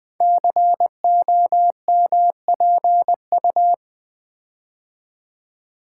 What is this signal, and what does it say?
Morse code, heard as a keyed tone, about 20 wpm, 700 Hz. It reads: COMPU
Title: Build A Word - 5 Letters At A Time - 10-Letter Words - 20wpm